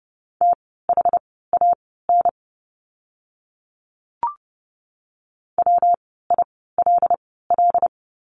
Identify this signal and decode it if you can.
Morse code, heard as a keyed tone, about 30 wpm, spaced as for 17 wpm, 700 Hz. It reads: THAN WILL
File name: course / 30:10wpm / 230 Lesson 37f - Letter x - Words 30wpm